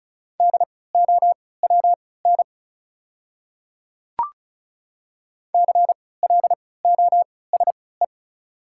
Morse code, heard as a keyed tone, about 36 wpm, spaced as for 20 wpm, 690 Hz. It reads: DOWN CLOSE
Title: 3x Character Spacing - Top 500 Words - 35wpm